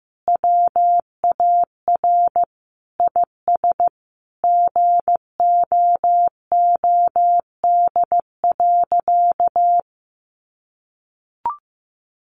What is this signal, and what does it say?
Morse code, heard as a keyed tone, about 15 wpm, 700 Hz. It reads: WAR IS GOOD.
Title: Sentences from Top 1300 Words 15wpm